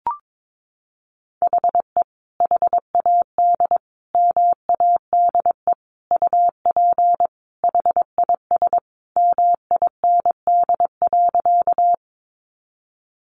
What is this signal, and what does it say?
Morse code, heard as a keyed tone, about 22 wpm, 700 Hz. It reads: HE HAD MADE UP HIS MIND.
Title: Sentences from Top 500 Words 22wpm